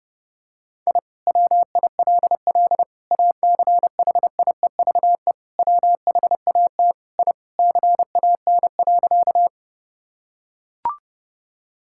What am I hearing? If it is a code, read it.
Morse code, heard as a keyed tone, about 30 wpm, 700 Hz. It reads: I WILL ACHIEVE WHAT I CAN.
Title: Sentences from Top 1000 Words - Review Part 6 - 30wpm